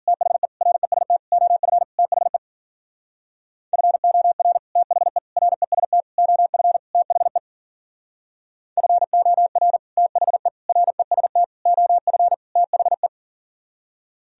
Morse code, heard as a keyed tone, about 52 wpm, 700 Hz. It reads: THE REST OF THE FOR THE REST OF THE FOR THE REST OF THE